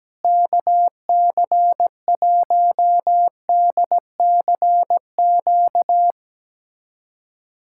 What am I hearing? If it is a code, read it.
Morse code, heard as a keyed tone, about 17 wpm, 700 Hz. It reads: KC1DCQ
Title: USA Callsign DE Callsign 17wpm